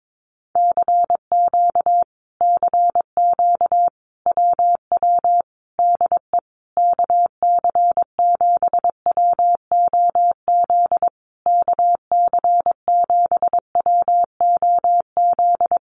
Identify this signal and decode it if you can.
Morse code, heard as a keyed tone, about 22 wpm, 700 Hz. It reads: CQ CQ WW DE KC7WOZ KC7WOZ
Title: Common QSO Phrases Encore 22wpm